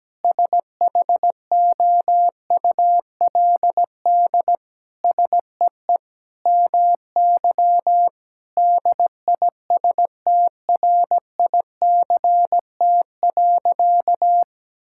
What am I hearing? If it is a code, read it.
Morse code, heard as a keyed tone, about 17 wpm, 700 Hz. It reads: SHOULD SEE MY DISTRICT.
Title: Sentences from Top 1600 Words 17wpm